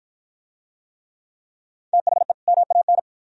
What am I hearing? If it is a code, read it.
Morse code, heard as a keyed tone, about 53 wpm, 700 Hz. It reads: THE MAN